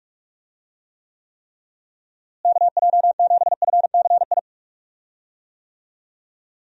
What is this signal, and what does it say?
Morse code, heard as a keyed tone, about 42 wpm, 700 Hz. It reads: KJ7FCI
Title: USA Call Signs - 6 Characters - Repeated 3x - 45wpm